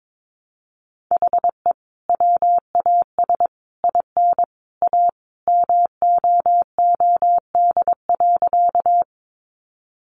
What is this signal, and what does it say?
Morse code, heard as a keyed tone, about 22 wpm, 700 Hz. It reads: HE WAS IN A MOOD.
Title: Sentences from Top 1100 Words 22wpm